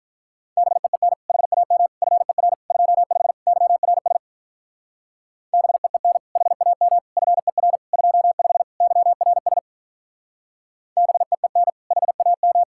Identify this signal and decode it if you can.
Morse code, heard as a keyed tone, about 43 wpm, 700 Hz. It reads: BEEN HAM FER 25 YRS BEEN HAM FER 25 YRS BEEN HAM
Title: Speed-Racing Common QSO Phrases - 35wpm